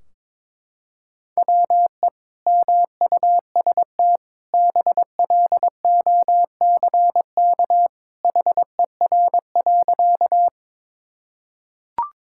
Morse code, heard as a keyed tone, about 22 wpm, 700 Hz. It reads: WE MUST BLOCK HER.
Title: Sentences from Top 1600 Words 22wpm